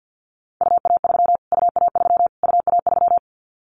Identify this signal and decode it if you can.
Morse code, heard as a keyed tone, about 50 wpm, 700 Hz. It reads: UA3 UA3 UA3
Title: Callsign Prefixes for Top 25 DXCC - Repeated 3x - 50wpm